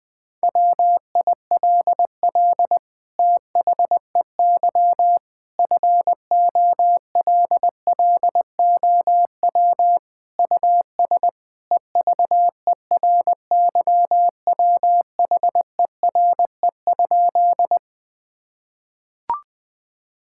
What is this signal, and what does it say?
Morse code, heard as a keyed tone, about 19 wpm, 700 Hz. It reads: WILL THEY FOLLOW US EVERYWHERE?
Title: Sentences from Top 1600 Words 20wpm